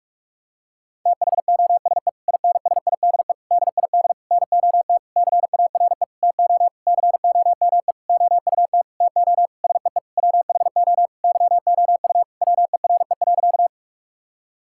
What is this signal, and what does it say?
Morse code, heard as a keyed tone, about 46 wpm, 700 Hz. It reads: THOSE INSIDE DID NOT CARE TO COME OUT TO SEE WHO YOU WERE.
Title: Sentences from Top 1000 Words - Review Part 6 - 45wpm